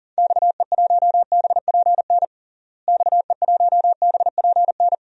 Morse code, heard as a keyed tone, about 40 wpm, 700 Hz. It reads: XE1BPN XE1BPN
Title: DXCC-100 Call Signs - 6 Characters - Repeated 2x - 40wpm